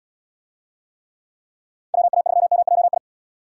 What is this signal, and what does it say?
Morse code, heard as a keyed tone, about 72 wpm, 700 Hz. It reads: WA3MJI